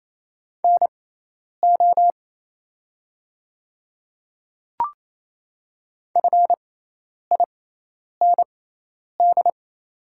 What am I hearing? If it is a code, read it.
Morse code, heard as a keyed tone, about 27 wpm, spaced as for 10 wpm, 700 Hz. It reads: NO FIND
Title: 6x Character Spacing - Top 100 Words - 28wpm